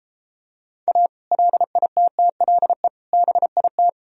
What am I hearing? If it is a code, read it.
Morse code, heard as a keyed tone, about 33 wpm, 700 Hz. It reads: A LITTLE BIT